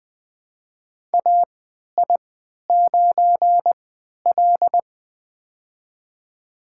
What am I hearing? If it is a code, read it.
Morse code, heard as a keyed tone, about 20 wpm, spaced as for 11 wpm, 700 Hz. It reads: AI9L
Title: USA Call Signs - 4 Characters - 3x Character Spacing - 20wpm